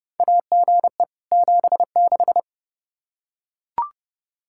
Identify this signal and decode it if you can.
Morse code, heard as a keyed tone, about 30 wpm, 700 Hz. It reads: AGE 76
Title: Practice QSO - Exchange 3 - More Info - 30wpm